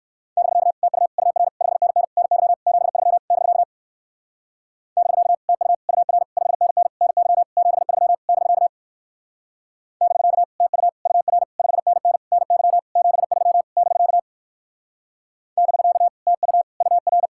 Morse code, heard as a keyed tone, about 56 wpm, 700 Hz. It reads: <BK> TU UR 5NN NY 73 <BK> <BK> TU UR 5NN NY 73 <BK> <BK> TU UR 5NN NY 73 <BK> <BK> TU UR
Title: Speed-Racing Common QSO Phrases - 45wpm